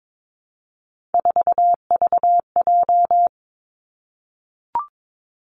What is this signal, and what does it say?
Morse code, heard as a keyed tone, about 22 wpm, 700 Hz. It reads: 4VJ
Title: Three Challenging Letter-Number Combinations 22wpm